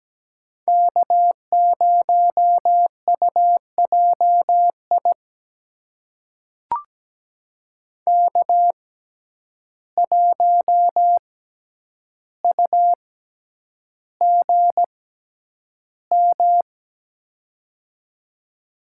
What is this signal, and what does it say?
Morse code, heard as a keyed tone, about 17 wpm, spaced as for 9 wpm, 700 Hz. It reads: K0UJI K1UGM
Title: USA Call Signs - 5 Characters - 6x Character Spacing - 17wpm